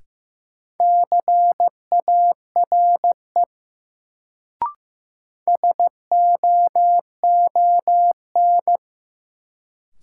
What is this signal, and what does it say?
Morse code, heard as a keyed tone, about 15 wpm, 700 Hz. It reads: CARE SOON
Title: Sets of 3 Words - New Word Drill - 15wpm